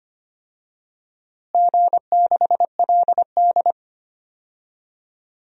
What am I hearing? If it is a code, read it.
Morse code, heard as a keyed tone, about 25 wpm, 700 Hz. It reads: G6LD
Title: DXCC-100 Call Signs - 4 Characters - Repeated 2x - 25wpm